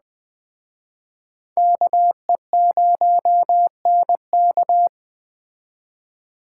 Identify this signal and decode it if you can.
Morse code, heard as a keyed tone, about 20 wpm, 700 Hz. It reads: KE0NK
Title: Speed Racing - USA 2x2 Call Signs 17wpm